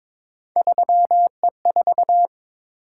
Morse code, heard as a keyed tone, about 22 wpm, 700 Hz. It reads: 3E4
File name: practice / 22wpm / Three Letter-Number Combinations - 22wpm